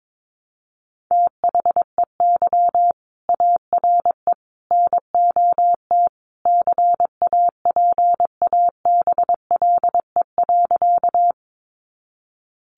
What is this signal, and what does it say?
Morse code, heard as a keyed tone, about 22 wpm, 700 Hz. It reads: THEY ARE NOT CAPABLE.
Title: Sentences from Top 1200 Words 22wpm